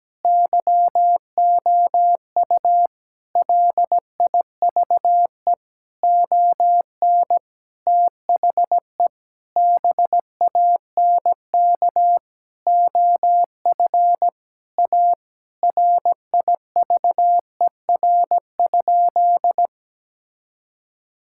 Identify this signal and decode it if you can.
Morse code, heard as a keyed tone, about 17 wpm, 700 Hz. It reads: YOU LIVE ON THE BANK OF A RIVER?
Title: Sentences from Top 1000 Words - Review Part 6 - 17wpm